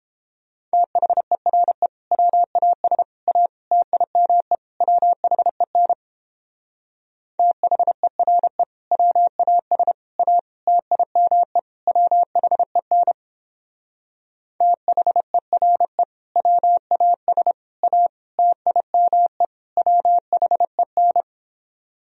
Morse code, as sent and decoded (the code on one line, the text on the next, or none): - .... .-. / .-- .- ... / .- / - .. -- . / .-- .... -. / - .... .-. / .-- .- ... / .- / - .. -- . / .-- .... -. / - .... .-. / .-- .- ... / .- / - .. -- . / .-- .... -.
THERE WAS A TIME WHEN THERE WAS A TIME WHEN THERE WAS A TIME WHEN